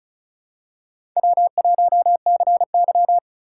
Morse code, heard as a keyed tone, about 34 wpm, 700 Hz. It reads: W1CY
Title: USA Call Signs - 4 Characters - 4x Character Spacing - 35wpm